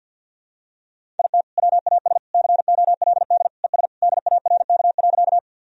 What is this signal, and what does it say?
Morse code, heard as a keyed tone, about 52 wpm, 700 Hz. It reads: IT WAS COLD ES DARK.
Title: Sentences from Top 500 Words and Abbreviations 50wpm